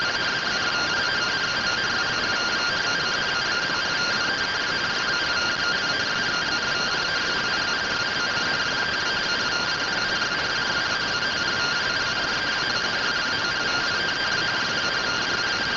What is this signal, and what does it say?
Two-tone teleprinter signal, about 170 Hz shift.